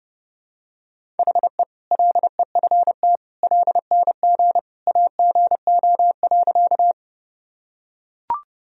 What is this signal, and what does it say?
Morse code, heard as a keyed tone, about 30 wpm, 700 Hz. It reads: HE LEFT LNG AGO.
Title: Sentences from Top 500 Words and Abbreviations 30wpm